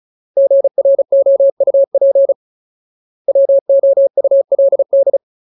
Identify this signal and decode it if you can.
Morse code, heard as a keyed tone, about 34 wpm, 550 Hz. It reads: GROUP WOULD
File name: Ultimate Word List / 35WPM Ultimate